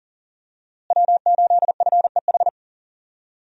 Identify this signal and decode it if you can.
Morse code, heard as a keyed tone, about 40 wpm, 700 Hz. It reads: W8FEH
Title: USA Call Signs - 5 Characters - Repeated 2x - 40wpm